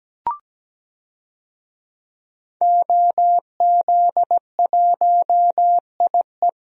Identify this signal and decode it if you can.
Morse code, heard as a keyed tone, about 17 wpm, 700 Hz. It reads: OZ1IE
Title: DXCC-100 Call Signs - 5 Characters - 17wpm